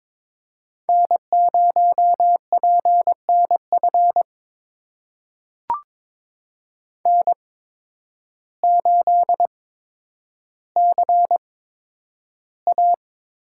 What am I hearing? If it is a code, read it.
Morse code, heard as a keyed tone, about 22 wpm, spaced as for 10 wpm, 700 Hz. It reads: N0PNF N8CA
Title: USA Call Signs - 5 Characters - 8x Character Spacing - 22wpm